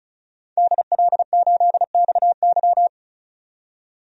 Morse code, heard as a keyed tone, about 36 wpm, 700 Hz. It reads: DL8XY
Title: DXCC-100 Call Signs - 5 Characters - 35wpm